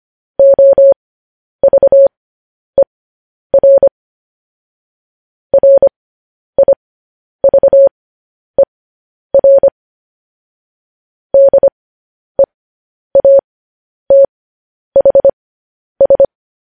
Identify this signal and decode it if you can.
Morse code, heard as a keyed tone, about 25 wpm, spaced as for 10 wpm, 550 Hz. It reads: OVER RIVER DEATHS
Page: news headlines in morse code at 10 wpm